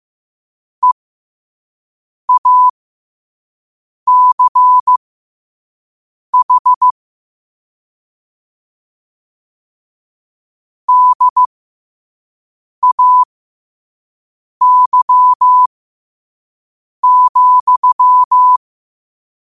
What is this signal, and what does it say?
Morse code, heard as a keyed tone, about 15 wpm, spaced as for 5 wpm, 1000 Hz. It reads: EACH DAY,
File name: 5wpm